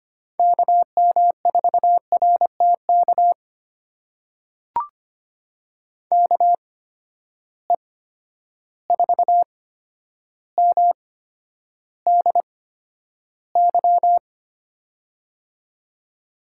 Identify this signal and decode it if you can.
Morse code, heard as a keyed tone, about 25 wpm, spaced as for 11 wpm, 700 Hz. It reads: KM4RTK KE4MDY
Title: USA Call Signs - 6 Characters - 8x Character Spacing - 25wpm